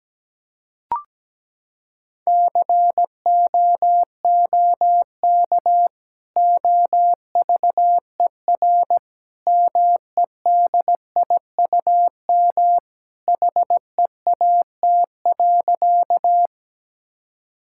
Morse code, heard as a keyed tone, about 17 wpm, 700 Hz. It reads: COOK OVER MEDIUM HEAT.
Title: Sentences from Top 1000 Words - Review Part 6 - 17wpm